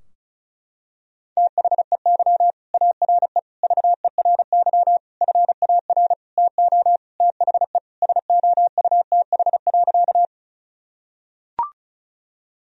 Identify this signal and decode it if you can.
Morse code, heard as a keyed tone, about 34 wpm, 700 Hz. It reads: THEY ARE VERY FAR TO THE SOUTH.
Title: Sentences from Top 1000 Words - Review Part 2 - 35wpm